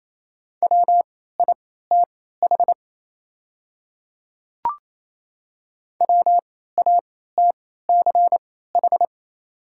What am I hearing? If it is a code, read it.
Morse code, heard as a keyed tone, about 28 wpm, spaced as for 16 wpm, 700 Hz. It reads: WITH WATCH
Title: 3x Character Spacing - Top 300 Words - 28wpm